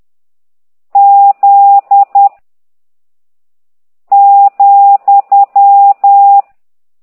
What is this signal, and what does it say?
Morse code, heard as a keyed tone, about 10 wpm, 800 Hz. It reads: Z ,